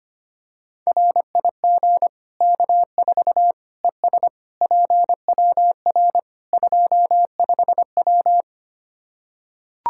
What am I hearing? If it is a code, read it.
Morse code, heard as a keyed tone, about 25 wpm, 700 Hz. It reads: RIG K4 ES PWR 25W
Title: Practice QSO - Exchange 2 - Info - 25wpm